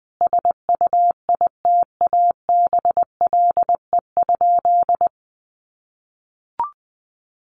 Morse code, heard as a keyed tone, about 20 wpm, 700 Hz. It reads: SUITABLE?
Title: Sentences from Top 1400 Words 20wpm